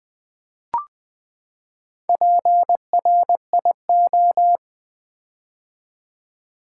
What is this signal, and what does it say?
Morse code, heard as a keyed tone, about 20 wpm, 700 Hz. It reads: PRIO
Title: Build A Word - 4 Letters At A Time - 8-Letter Words - 20wpm